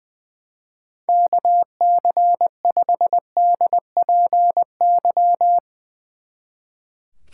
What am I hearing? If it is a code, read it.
Morse code, heard as a keyed tone, about 21 wpm, 700 Hz. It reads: KC5DPY
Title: Speed Racing - USA 2x3 Call Signs 20wpm